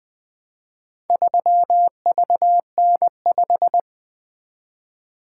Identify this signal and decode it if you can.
Morse code, heard as a keyed tone, about 20 wpm, 700 Hz. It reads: 3VN5
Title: Four Letter-Number Combinations - 20wpm